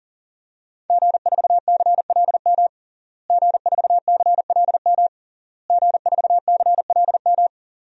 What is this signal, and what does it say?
Morse code, heard as a keyed tone, about 37 wpm, 700 Hz. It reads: G4CLM G4CLM G4CLM
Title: DXCC-100 Call Signs - 5 Characters - Repeated 3x - 40wpm